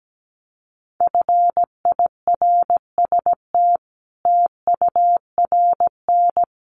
Morse code, heard as a keyed tone, about 17 wpm, 700 Hz. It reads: FIRST TURN
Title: Sentences from Top 300 Words 17wpm